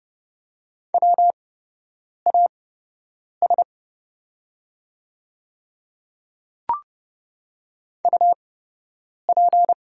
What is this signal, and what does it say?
Morse code, heard as a keyed tone, about 30 wpm, spaced as for 8 wpm, 700 Hz. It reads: WAS UP
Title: 8x Character Spacing - Top 100 Words - 30wpm